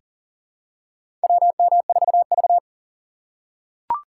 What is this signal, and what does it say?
Morse code, heard as a keyed tone, about 40 wpm, 700 Hz. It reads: WM4V
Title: USA Call Signs - 4 Characters - 2x Character Spacing - 40wpm